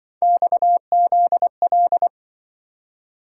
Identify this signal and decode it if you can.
Morse code, heard as a keyed tone, about 25 wpm, 700 Hz. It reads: XZL